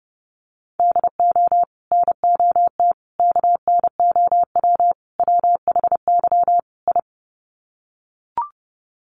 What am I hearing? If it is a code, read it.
Morse code, heard as a keyed tone, about 30 wpm, 700 Hz. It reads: DO NOT KNOW WHY I
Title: Sets of 5 Words Encore 30wpm